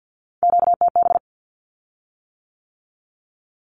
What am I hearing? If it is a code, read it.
Morse code, heard as a keyed tone, about 50 wpm, 700 Hz. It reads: QTB
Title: Q-Codes with Spoken Meaning - 50wpm